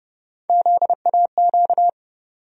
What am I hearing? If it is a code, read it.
Morse code, heard as a keyed tone, about 30 wpm, 700 Hz. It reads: ZAQ
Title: USA Call Sign Postfixes - 3 Letters 30wpm